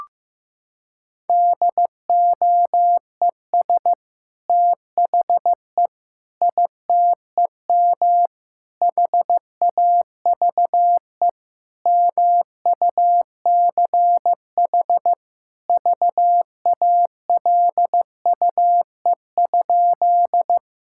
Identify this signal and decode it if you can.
Morse code, heard as a keyed tone, about 15 wpm, 700 Hz. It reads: DOES THE ITEM HAVE MUCH VALUE?
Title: Sentences from Top 1000 Words - Review Part 6 - 15wpm